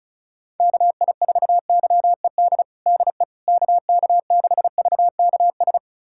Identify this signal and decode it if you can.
Morse code, heard as a keyed tone, about 36 wpm, 700 Hz. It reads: KI4YED DE KK6VKS
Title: USA Callsign DE Callsign - 6 Characters - 35wpm